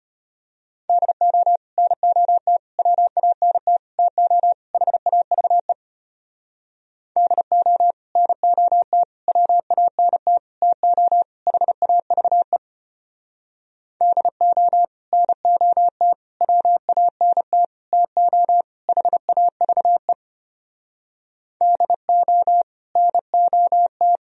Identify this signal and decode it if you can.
Morse code, heard as a keyed tone, about 33 wpm, 700 Hz. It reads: DO NOT WANT TO HAVE DO NOT WANT TO HAVE DO NOT WANT TO HAVE DO NOT